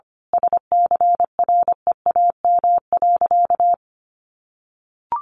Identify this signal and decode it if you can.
Morse code, heard as a keyed tone, about 25 wpm, 700 Hz. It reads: SCREAM.
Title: A Sound Of Thunder 25wpm